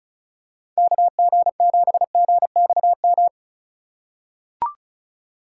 Morse code, heard as a keyed tone, about 34 wpm, 700 Hz. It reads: KG7GXM
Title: USA Call Signs - 6 Characters - 4x Character Spacing - 35wpm